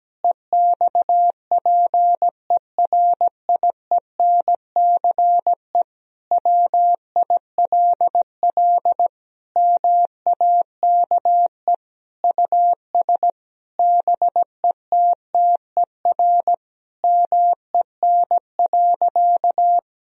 Morse code, heard as a keyed tone, about 17 wpm, 700 Hz. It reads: EXPERIENCE WILL MAKE US BETTER MEN.
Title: Sentences from Top 1000 Words - Review Part 3 - 17wpm